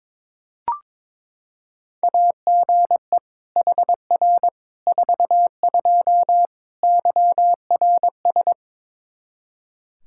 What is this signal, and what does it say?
Morse code, heard as a keyed tone, about 22 wpm, 700 Hz. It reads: AGE HR 42 YRS
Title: Practice QSO - Exchange 3 - More Info - 22wpm